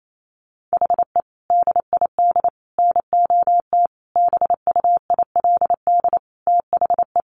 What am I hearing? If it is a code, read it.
Morse code, heard as a keyed tone, about 28 wpm, 700 Hz. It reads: HE DID NOT BUILD THE
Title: Sentences from Top 1200 Words 28wpm